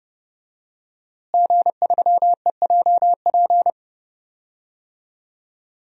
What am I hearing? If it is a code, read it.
Morse code, heard as a keyed tone, about 30 wpm, 700 Hz. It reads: G3EJP
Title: DXCC-100 Call Signs - 5 Characters - Repeated 3x - 30wpm